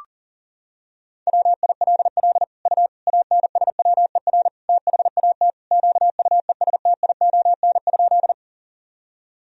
Morse code, heard as a keyed tone, about 40 wpm, 700 Hz. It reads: WILL U ANSWER THAT QUESTION?
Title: Sentences from Top 400 Words and Abbreviations 40wpm